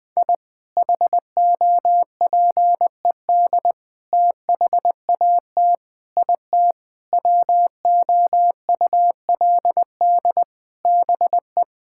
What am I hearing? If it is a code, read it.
Morse code, heard as a keyed tone, about 20 wpm, 700 Hz. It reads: I HOPED THAT IT WOULD BE
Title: Sentences with 4-Word Phrases - 20wpm